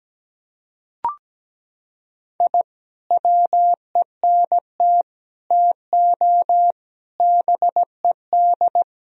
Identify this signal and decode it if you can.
Morse code, heard as a keyed tone, about 17 wpm, 700 Hz. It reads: I WENT TO BED
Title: Sentences from Top 1000 Words - Review Part 4 - 17wpm